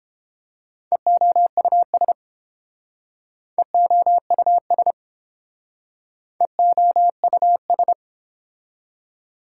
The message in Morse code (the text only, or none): EOUS EOUS EOUS